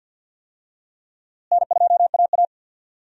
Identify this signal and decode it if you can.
Morse code, heard as a keyed tone, about 51 wpm, 700 Hz. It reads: N2AA
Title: USA Call Signs - 4 Characters - Repeated 3x - 50wpm